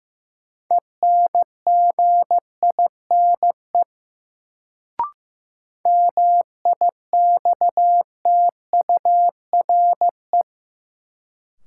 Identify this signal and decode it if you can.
Morse code, heard as a keyed tone, about 16 wpm, 700 Hz. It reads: ENGINE MIXTURE
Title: Sentences from Top 1500 Words - New Word Drill - 15wpm